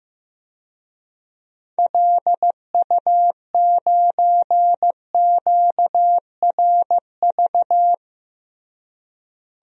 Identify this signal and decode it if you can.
Morse code, heard as a keyed tone, about 15 wpm, 700 Hz. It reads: LU9QRV
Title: DXCC-100 Call Signs - 6 Characters - Repeated 2x - 15wpm